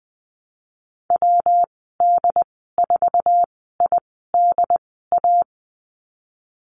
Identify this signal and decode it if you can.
Morse code, heard as a keyed tone, about 20 wpm, spaced as for 15 wpm, 700 Hz. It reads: WD4IDA
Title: 193 Lesson 31g - Letter v - Call Signs 20wpm